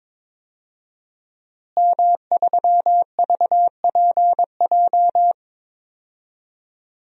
Morse code, heard as a keyed tone, about 22 wpm, 700 Hz. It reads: M3VPJ
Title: DXCC-100 Call Signs - 5 Characters - Repeated 2x - 22wpm